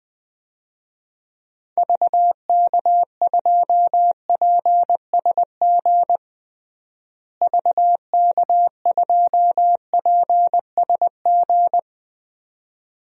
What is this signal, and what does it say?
Morse code, heard as a keyed tone, about 21 wpm, 700 Hz. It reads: VK2PSG VK2PSG
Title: DXCC-100 Call Signs - 6 Characters - Repeated 2x - 20wpm